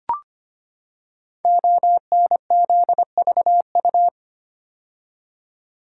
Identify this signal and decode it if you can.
Morse code, heard as a keyed tone, about 25 wpm, 700 Hz. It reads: ONZVU
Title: Code Groups - 5 Letter:Numbers - 25wpm